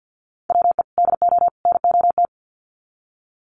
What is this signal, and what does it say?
Morse code, heard as a keyed tone, about 50 wpm, 700 Hz. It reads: WE DO NOT
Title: Sets of 3 Words 50wpm